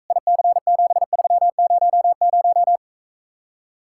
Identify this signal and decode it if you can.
Morse code, heard as a keyed tone, about 45 wpm, 700 Hz. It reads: IC7300